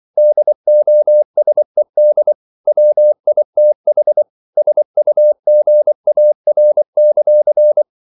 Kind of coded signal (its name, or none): Morse code